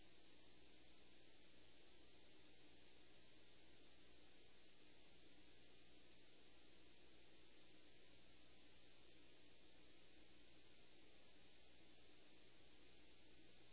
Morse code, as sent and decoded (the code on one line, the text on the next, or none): none